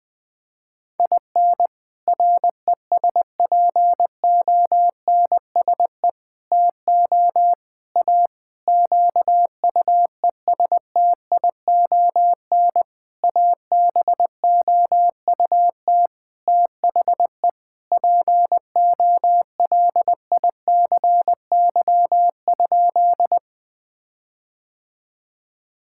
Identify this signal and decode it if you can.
Morse code, heard as a keyed tone, about 21 wpm, 700 Hz. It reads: IN RESPONSE TO A QUESTION ABOUT THE POLICY?
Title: Sentences with 6-Word Phrases - 20wpm